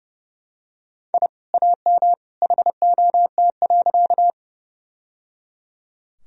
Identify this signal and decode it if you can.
Morse code, heard as a keyed tone, about 30 wpm, 700 Hz. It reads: I AM HOT.
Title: Sentences from Top 1000 Words - Review Part 3 - 30wpm